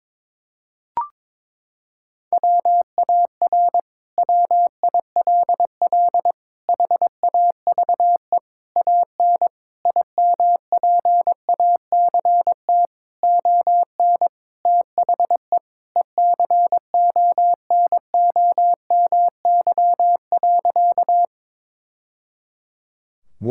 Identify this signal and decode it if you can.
Morse code, heard as a keyed tone, about 22 wpm, 700 Hz. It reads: WAR WILL HAVE AN IMPACT ON THE ECONOMY.
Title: Sentences from Top 1000 Words - Review Part 6 - 22wpm